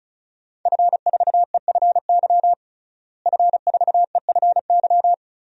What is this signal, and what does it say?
Morse code, heard as a keyed tone, about 36 wpm, 700 Hz. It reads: F4EFY F4EFY
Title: DXCC-100 Call Signs - 5 Characters - Repeated 3x - 35wpm